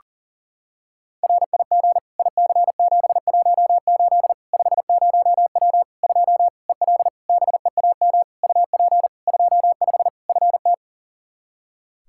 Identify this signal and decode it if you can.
Morse code, heard as a keyed tone, about 37 wpm, 700 Hz. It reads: RIG IC718 50W 2 EL BEAM UP 25 FT
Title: K1USN SST QSO Practice 40wpm